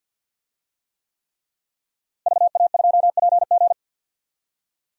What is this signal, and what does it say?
Morse code, heard as a keyed tone, about 51 wpm, 700 Hz. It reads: VA2PG
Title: DXCC-100 Call Signs - 5 Characters - Repeated 3x - 50wpm